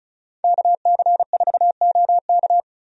Morse code, heard as a keyed tone, about 36 wpm, 700 Hz. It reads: KC4OK
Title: USA Call Signs - 5 Characters - 35wpm